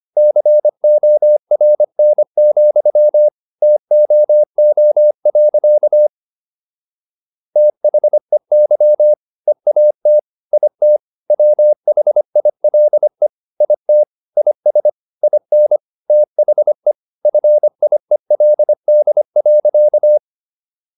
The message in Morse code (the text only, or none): CORN, TOO. THEY EAT IT WHILE IT IS IN THE FIELD.